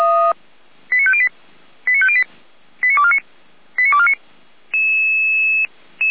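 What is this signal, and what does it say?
Five-tone selective call (ZVEI), heard as quick runs of five tones, tones of about 70 ms. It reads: 89508 89508 89249 89249
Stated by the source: ZVEI five-tone alert received by SDR, audio.f85360.feuerwehr-orlamuende.org fire brigade server.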